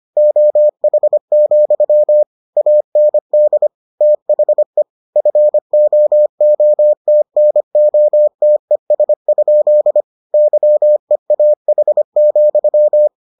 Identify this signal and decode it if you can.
Morse code, heard as a keyed tone, about 25 wpm, 600 Hz. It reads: OH, AND THE FOOTNOTES? YEAH,